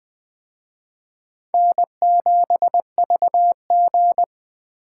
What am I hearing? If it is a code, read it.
Morse code, heard as a keyed tone, about 20 wpm, 700 Hz. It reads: N7VG
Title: Speed Racing - USA 1x2 Call Signs 17wpm